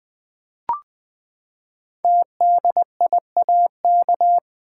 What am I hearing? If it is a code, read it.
Morse code, heard as a keyed tone, about 20 wpm, 700 Hz. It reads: TDIAK
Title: Code Groups - 5 Letters - 20wpm